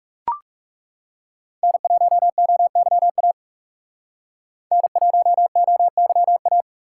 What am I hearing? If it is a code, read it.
Morse code, heard as a keyed tone, about 36 wpm, 700 Hz. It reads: N1OYA N1OYA
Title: Speed Racing - USA 1x3 Call Signs 30wpm